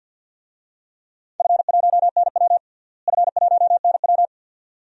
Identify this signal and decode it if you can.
Morse code, heard as a keyed tone, about 50 wpm, 700 Hz. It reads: F1NW F1NW
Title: DXCC-100 Call Signs - 4 Characters - Repeated 2x - 50wpm